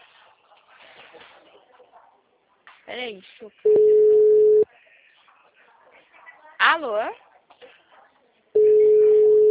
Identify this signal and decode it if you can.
Ringback tone (Europe, 425 Hz), with speech.